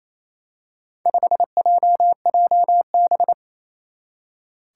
Morse code, heard as a keyed tone, about 27 wpm, 700 Hz. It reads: 5JJB
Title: Four Challenging Letter-Number Combinations 28wpm